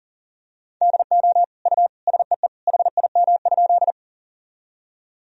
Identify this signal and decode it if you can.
Morse code, heard as a keyed tone, about 40 wpm, 700 Hz. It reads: DO U SEE HIM?
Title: Sentences from Top 100 Words and Abbreviations 40wpm